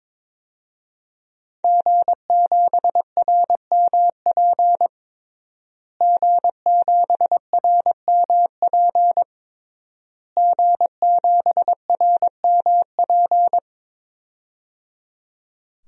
Morse code, heard as a keyed tone, about 21 wpm, 700 Hz. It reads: G7RMP G7RMP G7RMP
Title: DXCC-100 Call Signs - 5 Characters - Repeated 3x - 22wpm